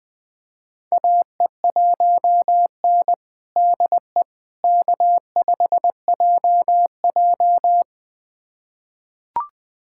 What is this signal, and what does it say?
Morse code, heard as a keyed tone, about 21 wpm, 700 Hz. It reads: AE1N DE K5JJ
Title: USA Callsign DE Callsign - 4 Characters - 20wpm